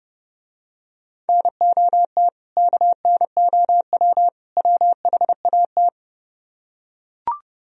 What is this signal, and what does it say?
Morse code, heard as a keyed tone, about 30 wpm, 700 Hz. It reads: NOT KNOW WHAT